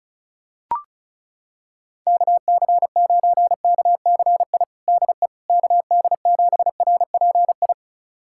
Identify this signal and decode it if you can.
Morse code, heard as a keyed tone, about 36 wpm, 700 Hz. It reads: KC9KCI DE KD7RPI
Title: USA Callsign DE Callsign 35wpm